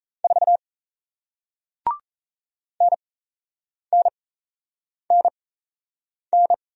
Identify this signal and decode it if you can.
Morse code, heard as a keyed tone, about 37 wpm, spaced as for 8 wpm, 700 Hz. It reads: 4 NNNN